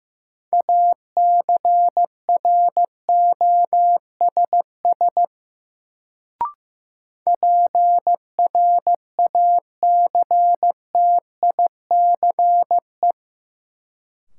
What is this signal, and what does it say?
Morse code, heard as a keyed tone, about 15 wpm, 700 Hz. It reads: ACROSS PRACTICE
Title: Sentences from Top 700 Words - New Word Drill - 15wpm